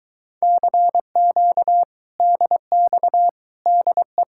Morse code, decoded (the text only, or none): CQ DX DE